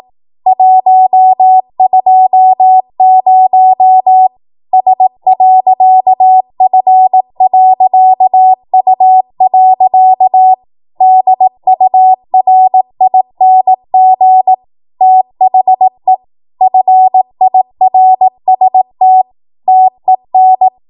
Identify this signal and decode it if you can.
Morse code, heard as a keyed tone, about 18 wpm, 750 Hz. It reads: 120 S.F.U. DURING THE FIRST TEN
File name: Morse / 18WPM